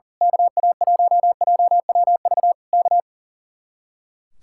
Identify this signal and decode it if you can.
Morse code, heard as a keyed tone, about 38 wpm, 700 Hz. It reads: KA1JWV K
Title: Speed-Racing Common QSO Phrases - 40wpm